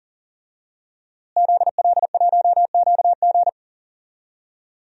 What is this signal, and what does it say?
Morse code, heard as a keyed tone, about 37 wpm, 700 Hz. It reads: ZL1QG
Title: DXCC-100 Call Signs - 5 Characters - Repeated 2x - 40wpm